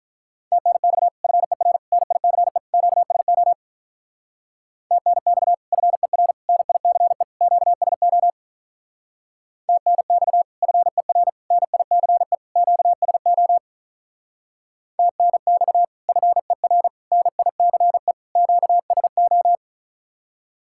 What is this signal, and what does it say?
Morse code, heard as a keyed tone, about 46 wpm, 700 Hz. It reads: TNX FER NICE QSO TNX FER NICE QSO TNX FER NICE QSO TNX FER NICE QSO